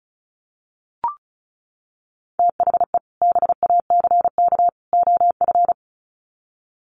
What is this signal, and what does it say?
Morse code, heard as a keyed tone, about 34 wpm, 700 Hz. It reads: THE BACK OF
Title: Sets of 3 Words 35wpm